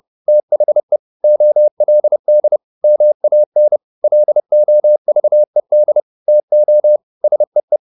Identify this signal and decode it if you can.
Morse code, heard as a keyed tone, about 30 wpm, 600 Hz. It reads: THE OLD MAN LOVED TO SEE
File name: the old man and the sea / the old man and the sea 1 30wpm